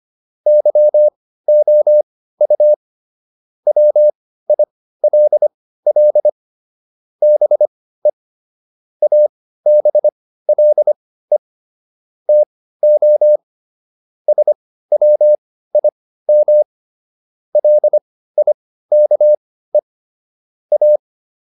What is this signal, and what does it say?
Morse code, heard as a keyed tone, about 25 wpm, spaced as for 15 wpm, 600 Hz. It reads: YOU WILL BE ABLE TO SWIM LIKE A